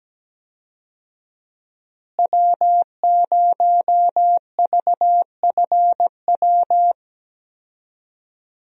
Morse code, heard as a keyed tone, about 17 wpm, 700 Hz. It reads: W0VFW
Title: USA Call Signs - 5 Characters - Repeated 2x - 17wpm